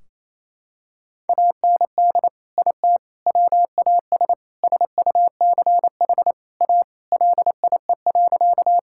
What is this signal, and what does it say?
Morse code, heard as a keyed tone, about 28 wpm, 700 Hz. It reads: AND IT WAS SUCH A LIE.
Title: Sentences from Top 1600 Words 28wpm